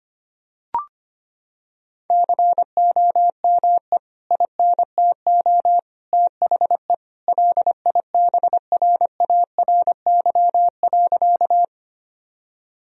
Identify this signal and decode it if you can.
Morse code, heard as a keyed tone, about 25 wpm, 700 Hz. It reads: COME INTO THE LIBRARY.
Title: Sentences from Top 1000 Words - Review Part 4 - 25wpm